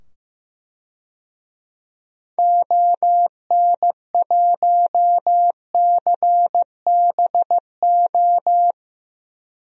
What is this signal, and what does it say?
Morse code, heard as a keyed tone, about 15 wpm, 700 Hz. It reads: ON1CBO
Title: DXCC-100 Call Signs - 6 Characters - 15wpm